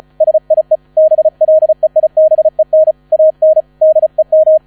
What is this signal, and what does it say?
Morse code, heard as a keyed tone, about 34 wpm, 630 Hz. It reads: SIE BLEIBEN AN DEM
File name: 35wpm